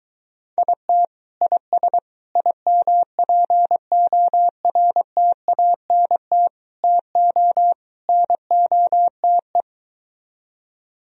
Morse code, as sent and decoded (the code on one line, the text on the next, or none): .. - / .. ... / .. -- .--. --- .-. - .- -. - / - --- / -. --- - .
IT IS IMPORTANT TO NOTE